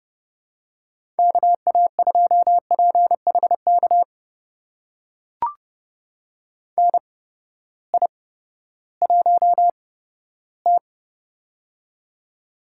Morse code, heard as a keyed tone, about 30 wpm, spaced as for 15 wpm, 700 Hz. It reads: KA2PHK NI1T
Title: USA Call Signs - 8x Character Spacing - 30wpm